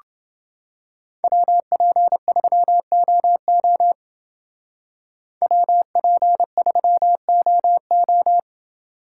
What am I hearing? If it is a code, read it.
Morse code, heard as a keyed tone, about 28 wpm, 700 Hz. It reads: WP3OO WP3OO